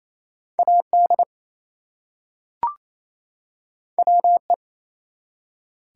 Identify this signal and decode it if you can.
Morse code, heard as a keyed tone, about 28 wpm, 700 Hz. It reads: AD WE
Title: Two Letter Words - 28wpm